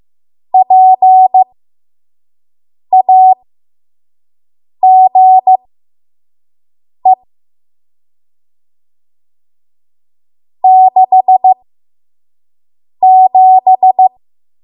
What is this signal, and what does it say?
Morse code, heard as a keyed tone, about 15 wpm, spaced as for 5 wpm, 750 Hz. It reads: PAGE 67
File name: Morse / 05WPM